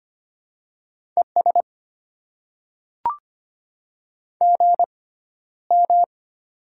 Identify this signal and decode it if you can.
Morse code, heard as a keyed tone, about 25 wpm, spaced as for 13 wpm, 700 Hz. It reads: ES GM
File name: Top 20 QSO Elements - 6x Character Spacing - 25wpm